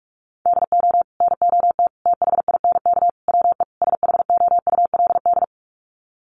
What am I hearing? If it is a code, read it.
Morse code, heard as a keyed tone, about 44 wpm, 700 Hz. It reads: DO NOT THINK WE SHOULD
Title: Sets of 5 Words Encore 45wpm